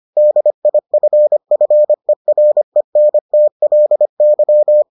Morse code, heard as a keyed tone, about 25 wpm, 600 Hz. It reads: DIFFERENTLY